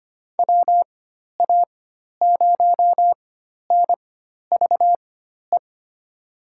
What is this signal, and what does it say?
Morse code, heard as a keyed tone, about 25 wpm, spaced as for 12 wpm, 700 Hz. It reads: WA0NVE
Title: USA Call Signs - 4x Character Spacing - 25wpm